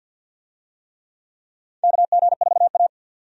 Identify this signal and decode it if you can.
Morse code, heard as a keyed tone, about 53 wpm, 700 Hz. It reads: KG4A
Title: USA Call Signs - 4 Characters - 50wpm